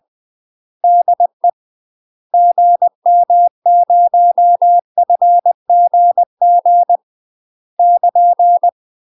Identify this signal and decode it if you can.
Morse code, heard as a keyed tone, about 20 wpm, 700 Hz. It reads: DE GM0FGG <KN>
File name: UK MASTER QSOs TRAINING / 20WPM / UK MASTER QSOs 0012